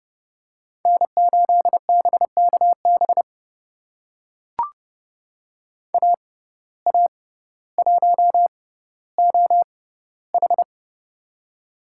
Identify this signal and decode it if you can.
Morse code, heard as a keyed tone, about 30 wpm, spaced as for 15 wpm, 700 Hz. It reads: N8BKB AA1OH
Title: USA Call Signs - 5 Characters - 6x Character Spacing - 30wpm